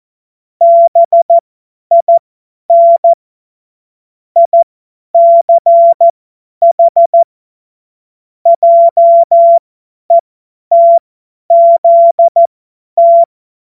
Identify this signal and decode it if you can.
Morse code, heard as a keyed tone, about 14 wpm, spaced as for 10 wpm, 680 Hz. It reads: BIN ICH JETZT
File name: Morsekurs / Witze / Witz 05 März 2024 50 F70